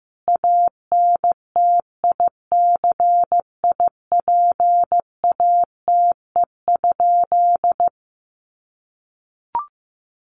Morse code, heard as a keyed tone, about 15 wpm, 700 Hz. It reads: ANTICIPATE?
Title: Sentences from Top 1500 Words 15wpm